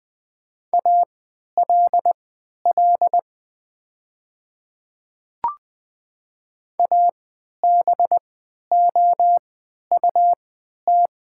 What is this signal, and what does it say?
Morse code, heard as a keyed tone, about 20 wpm, spaced as for 11 wpm, 700 Hz. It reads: ALL ABOUT
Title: 3x Character Spacing - Top 100 Words - 20wpm